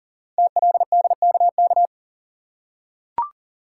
Morse code, heard as a keyed tone, about 42 wpm, 700 Hz. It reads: TLDKK
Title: Code Groups - 5 Letters - 40wpm